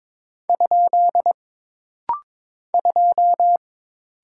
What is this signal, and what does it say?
Morse code, heard as a keyed tone, about 22 wpm, 700 Hz. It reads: ? 2
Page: This rapid-fire timing is geared to push your ICR (Instant Character Recognition) skills.